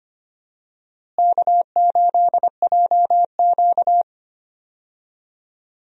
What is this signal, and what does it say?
Morse code, heard as a keyed tone, about 25 wpm, 700 Hz. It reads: K8JQ
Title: USA Call Signs - 4 Characters - Repeated 2x - 25wpm